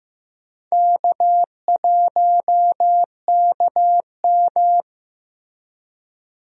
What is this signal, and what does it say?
Morse code, heard as a keyed tone, about 15 wpm, 700 Hz. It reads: K1KM
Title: USA Call Signs - 4 Characters - Repeated 2x - 15wpm